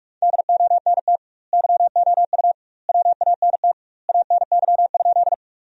Morse code, heard as a keyed tone, about 46 wpm, 700 Hz. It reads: DONT YOU WANT ANY?
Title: 2x Word-Spacing - Sentences from Top 300 Words - 45wpm